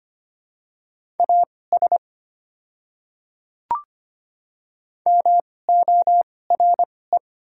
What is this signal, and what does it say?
Morse code, heard as a keyed tone, about 25 wpm, spaced as for 18 wpm, 700 Hz. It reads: AS MORE
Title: 2x Character Spacing - Top 100 Words - 25wpm